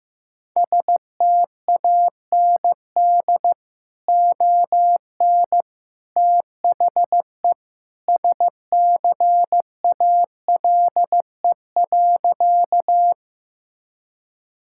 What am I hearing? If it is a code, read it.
Morse code, heard as a keyed tone, about 14 wpm, 700 Hz. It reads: STAND ON THE SCALE.
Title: Sentences from Top 1000 Words - Review Part 5 - 15wpm